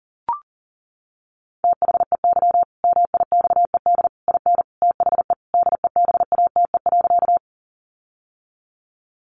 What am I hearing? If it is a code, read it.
Morse code, heard as a keyed tone, about 37 wpm, 700 Hz. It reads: THEY MIXED IN THE DEBATE.
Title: Sentences from Top 1500 Words 40wpm